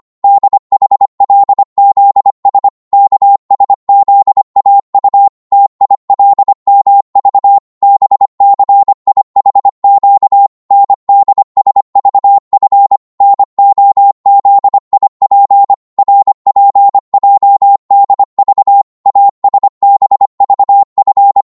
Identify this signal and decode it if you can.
Morse code, heard as a keyed tone, about 25 wpm, 800 Hz. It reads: DHLZS KSZAU TILMV BCIHQ NDSVF NOZIP RPJDV ASBVF